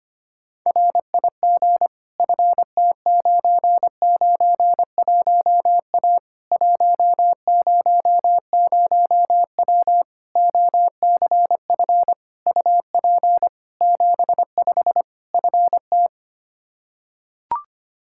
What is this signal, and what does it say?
Morse code, heard as a keyed tone, about 25 wpm, 700 Hz. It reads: RIG FT991A 100W OCF UP 75 FT
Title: K1USN SST QSO Practice 25wpm